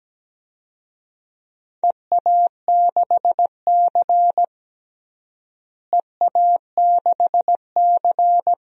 Morse code, heard as a keyed tone, about 17 wpm, 700 Hz. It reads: EA6C EA6C
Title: DXCC-100 Call Signs - 4 Characters - Repeated 2x - 17wpm